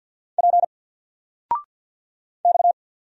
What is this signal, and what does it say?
Morse code, heard as a keyed tone, about 50 wpm, 700 Hz. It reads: P X
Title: Single Letter-Number 50wpm